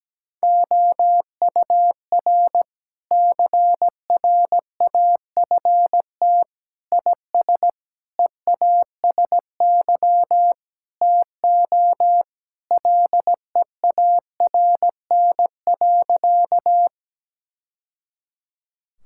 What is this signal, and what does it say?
Morse code, heard as a keyed tone, about 17 wpm, 700 Hz. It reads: OUR CRAFT IS EASY TO LEARN.
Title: Sentences from Top 1000 Words - Review Part 6 - 17wpm